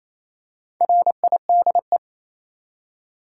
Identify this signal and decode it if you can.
Morse code, heard as a keyed tone, about 28 wpm, 700 Hz. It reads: RIDE
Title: Four Letter Words - 28wpm